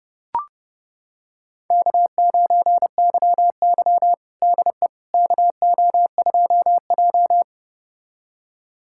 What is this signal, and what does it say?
Morse code, heard as a keyed tone, about 30 wpm, 700 Hz. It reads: K9YY DE KO2J
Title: USA Callsign DE Callsign - 4 Characters - 30wpm